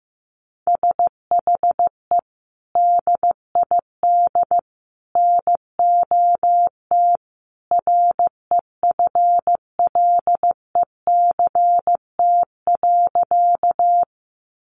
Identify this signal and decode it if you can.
Morse code, heard as a keyed tone, about 15 wpm, 700 Hz. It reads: SHE DID NOT REFLECT.
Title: Sentences from Top 1500 Words 15wpm